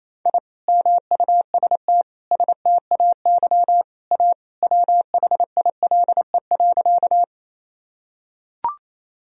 Morse code, heard as a keyed tone, about 28 wpm, 700 Hz. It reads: I MUST STAY A WHILE.
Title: Sentences from Top 1000 Words - Review Part 1 - 28wpm